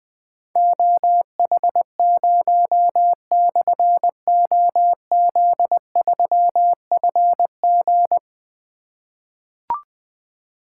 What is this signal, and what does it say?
Morse code, heard as a keyed tone, about 20 wpm, 700 Hz. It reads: OH0/OZ3FG
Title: Difficult Call Signs 20wpm